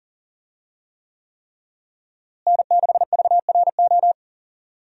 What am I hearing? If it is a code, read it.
Morse code, heard as a keyed tone, about 37 wpm, 700 Hz. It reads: N6VRO
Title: USA Call Signs - 5 Characters - Repeated 3x - 40wpm